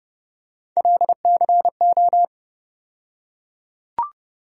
Morse code, heard as a keyed tone, about 30 wpm, 700 Hz. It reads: LCO
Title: USA Call Sign Postfixes - 3 Letters 30wpm